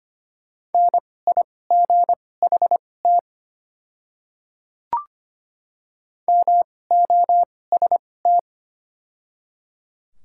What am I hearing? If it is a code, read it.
Morse code, heard as a keyed tone, about 25 wpm, spaced as for 18 wpm, 690 Hz. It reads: NIGHT MOST
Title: 2x Character Spacing - Top 300 Words - 25wpm